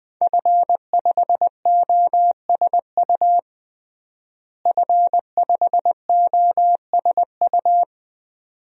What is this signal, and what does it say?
Morse code, heard as a keyed tone, about 20 wpm, 700 Hz. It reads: F5OSU F5OSU
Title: DXCC-100 Call Signs - 5 Characters - Repeated 2x - 20wpm